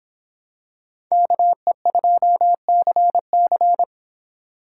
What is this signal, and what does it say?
Morse code, heard as a keyed tone, about 25 wpm, 700 Hz. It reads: KE2CC